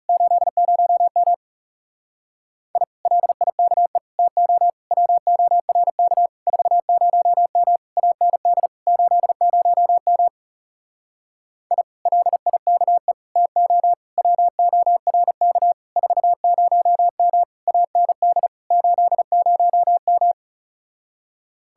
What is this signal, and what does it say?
Morse code, heard as a keyed tone, about 40 wpm, 700 Hz. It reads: 80M I LIKE TO WORK 40M AND 80M I LIKE TO WORK 40M AND 80M